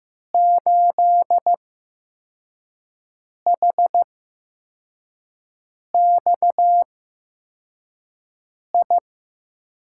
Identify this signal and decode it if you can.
Morse code, heard as a keyed tone, about 15 wpm, spaced as for 4 wpm, 700 Hz. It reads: 8HXI
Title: USA Call Signs - 6 Characters - 8x Character Spacing - 15wpm